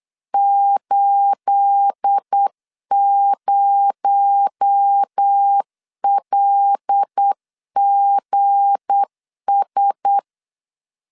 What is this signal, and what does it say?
Morse code, heard as a keyed tone, about 8 wpm, 800 Hz. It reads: 80LGS
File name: LEZIONI / 10 SIGNS / 10A SLASH 8wpm 3452